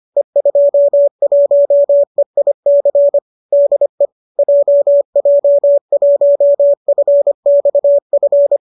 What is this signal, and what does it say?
Morse code, heard as a keyed tone, about 25 wpm, 570 Hz. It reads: E21EIC DE JJ1FXF
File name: qso208 25